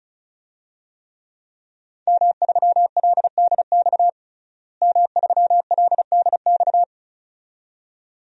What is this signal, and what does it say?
Morse code, heard as a keyed tone, about 34 wpm, 700 Hz. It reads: M3LDX M3LDX
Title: DXCC-100 Call Signs - 5 Characters - Repeated 2x - 35wpm